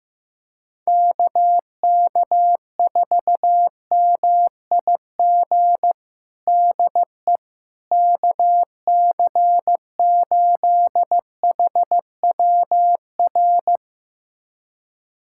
Morse code, heard as a keyed tone, about 15 wpm, 700 Hz. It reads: KK4MIG DE KC8HWR
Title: USA Callsign DE Callsign 15wpm